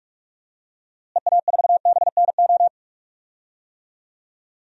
Morse code, heard as a keyed tone, about 46 wpm, 700 Hz. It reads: EA4BNO
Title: DXCC-100 Call Signs - 6 Characters - Repeated 2x - 45wpm